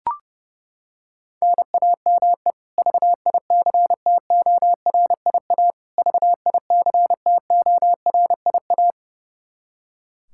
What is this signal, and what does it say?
Morse code, heard as a keyed tone, about 30 wpm, 700 Hz. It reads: NAME VICTORIA VICTORIA
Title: K1USN SST QSO Practice 30wpm